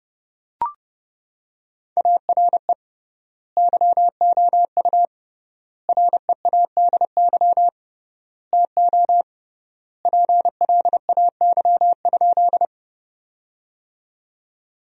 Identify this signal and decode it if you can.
Morse code, heard as a keyed tone, about 31 wpm, 700 Hz. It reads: ARE YOU READY TO PLAY?
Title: 3x Word-Spacing - Sentences from Top 400 Words - 30wpm